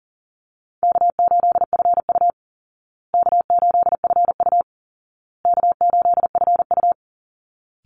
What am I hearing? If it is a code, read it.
Morse code, heard as a keyed tone, about 40 wpm, 700 Hz. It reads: K8FU K8FU K8FU
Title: Easy Call Signs Repeated 3x - 40wpm